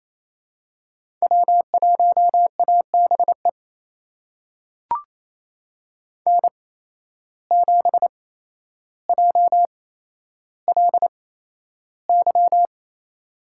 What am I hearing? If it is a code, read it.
Morse code, heard as a keyed tone, about 28 wpm, spaced as for 12 wpm, 700 Hz. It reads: W1ABE N7JLY
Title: USA Call Signs - 8x Character Spacing - 28wpm